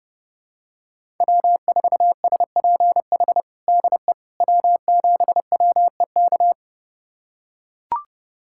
Morse code, heard as a keyed tone, about 30 wpm, 700 Hz. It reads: W4SPH DE W7WEK
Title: USA Callsign DE Callsign - 5 Characters - 30wpm